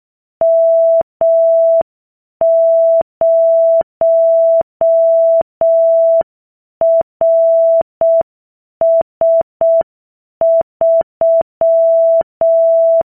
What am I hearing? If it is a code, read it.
Morse code, heard as a keyed tone, about 6 wpm, 660 Hz. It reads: M0RS3